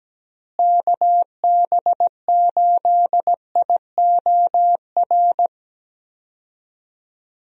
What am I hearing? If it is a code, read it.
Morse code, heard as a keyed tone, about 17 wpm, 700 Hz. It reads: KB8IOR
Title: USA Call Signs - 6 Characters - Repeated 3x - 17wpm